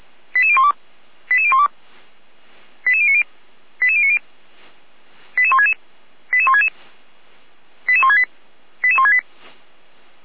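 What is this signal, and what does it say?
Five-tone selective call (ZVEI), heard as quick runs of five tones, tones of about 70 ms. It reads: 88012 88012 88090 88090 88166 88166 88168 88168
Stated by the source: ZVEI five-tone call 88427 received on an SDR.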